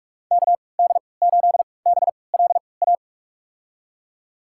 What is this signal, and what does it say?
Morse code, heard as a keyed tone, about 47 wpm, spaced as for 26 wpm, 700 Hz. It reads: KD8BLA
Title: USA Call Signs - 3x Character Spacing - 45wpm